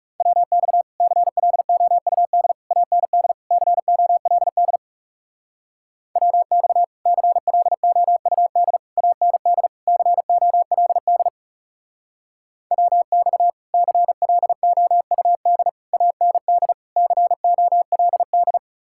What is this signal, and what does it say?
Morse code, heard as a keyed tone, about 37 wpm, 700 Hz. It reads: WX CLOUD AND COLD WX CLOUD AND COLD WX CLOUD AND COLD
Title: Speed-Racing Common QSO Phrases - 30wpm